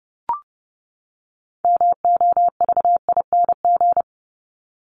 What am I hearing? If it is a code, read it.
Morse code, heard as a keyed tone, about 30 wpm, 700 Hz. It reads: MOVING
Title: ING Suffix - Top 100 Words with Base Words 30wpm